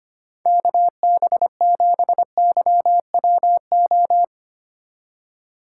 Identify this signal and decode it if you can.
Morse code, heard as a keyed tone, about 25 wpm, 700 Hz. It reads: KB7YWO
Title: USA Call Signs - 6 Characters - 3x Character Spacing - 25wpm